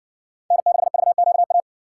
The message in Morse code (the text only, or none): N6VQA